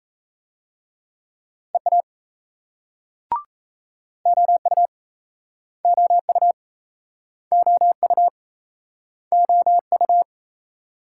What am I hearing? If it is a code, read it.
Morse code, heard as a keyed tone, about 40 wpm, 700 Hz. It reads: EA OU OU OU OU